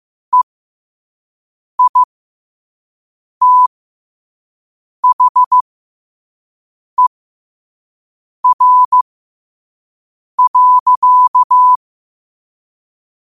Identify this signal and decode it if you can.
Morse code, heard as a keyed tone, about 15 wpm, spaced as for 5 wpm, 1000 Hz. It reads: EITHER.
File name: qotdfiles / 5wpm